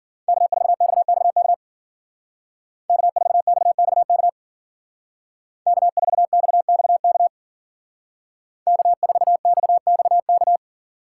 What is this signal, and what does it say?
Morse code, heard as a keyed tone, about 48 wpm, 700 Hz. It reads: K4XXK K4XXK K4XXK K4XXK
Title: Speed Racing - USA 1x3 Call Signs 40wpm